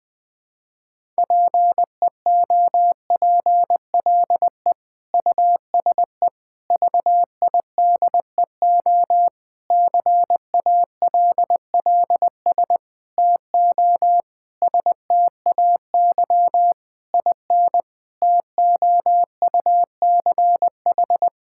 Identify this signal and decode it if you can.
Morse code, heard as a keyed tone, about 20 wpm, 700 Hz. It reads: PEOPLE USE VIDEO CALLS TO STAY IN TOUCH
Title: Difficult Sentences with Hints - 20wpm